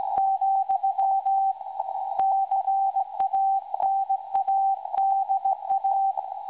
Morse code, heard as a keyed tone, about 18 wpm, 780 Hz. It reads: , CANADA
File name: WebSDR records / SAQ recording 17.2kHz